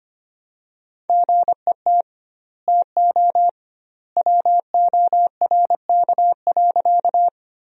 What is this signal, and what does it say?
Morse code, heard as a keyed tone, about 25 wpm, 700 Hz. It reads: GET TO WORK.
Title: 2x Word-Spacing - Sentences from Top 200 Words - 25wpm